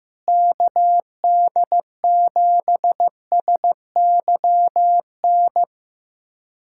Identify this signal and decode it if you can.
Morse code, heard as a keyed tone, about 15 wpm, 700 Hz. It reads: KD7SYN
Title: USA Callsign DE Callsign 15wpm